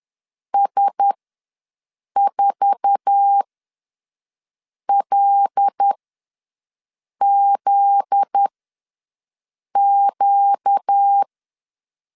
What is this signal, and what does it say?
Morse code, heard as a keyed tone, about 10 wpm, spaced as for 5 wpm, 800 Hz. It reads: S4LZQ
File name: LEZIONI / 8 YJ45 / 8 YJ4 ws 10wpm 4020